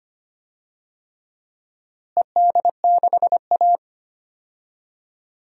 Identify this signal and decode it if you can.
Morse code, heard as a keyed tone, about 25 wpm, 700 Hz. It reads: ED6A
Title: DXCC-100 Call Signs - 4 Characters - Repeated 3x - 25wpm